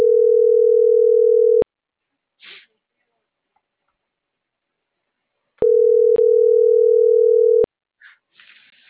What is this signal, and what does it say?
Ringback tone (North America), 440 + 480 Hz.